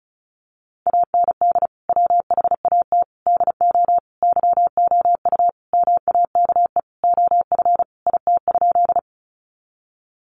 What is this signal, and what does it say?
Morse code, heard as a keyed tone, about 34 wpm, 700 Hz. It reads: AND WHAT DO YOU MAKE OF IT?
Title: Sentences from Top 100 Words 35wpm